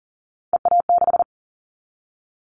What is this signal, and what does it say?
Morse code, heard as a keyed tone, about 40 wpm, 700 Hz. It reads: EA6
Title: Callsign Prefixes for Top 50 DXCC - 40wpm